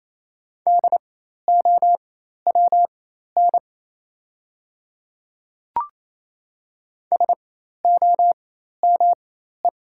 Morse code, heard as a keyed tone, about 28 wpm, spaced as for 13 wpm, 700 Hz. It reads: DOWN SOME
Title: 4x Character Spacing - Top 100 Words - 28wpm